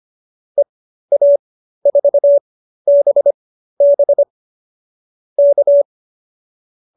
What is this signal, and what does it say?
Morse code, heard as a keyed tone, about 25 wpm, spaced as for 13 wpm, 590 Hz. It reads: EA4BB K
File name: qso215b 13